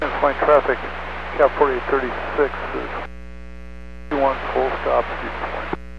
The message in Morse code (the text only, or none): none